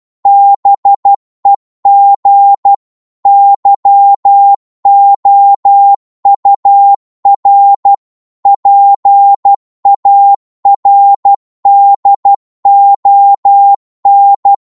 Morse code, heard as a keyed tone, about 12 wpm, 800 Hz. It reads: BEG YOUR PARDON
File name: Morse / psg-12wpm